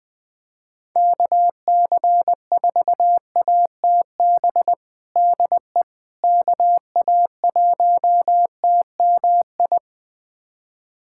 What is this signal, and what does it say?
Morse code, heard as a keyed tone, about 20 wpm, 700 Hz. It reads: KC4ATB DE KA1TMI
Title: USA Callsign DE Callsign - 6 Characters - 20wpm